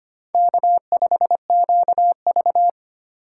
Morse code, heard as a keyed tone, about 25 wpm, 700 Hz. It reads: K5QV
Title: USA Call Signs - 4 Characters - 2x Character Spacing - 25wpm